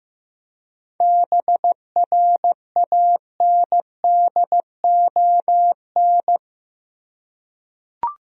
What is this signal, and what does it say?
Morse code, heard as a keyed tone, about 16 wpm, 700 Hz. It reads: BRANDON
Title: USA Common Names - 15wpm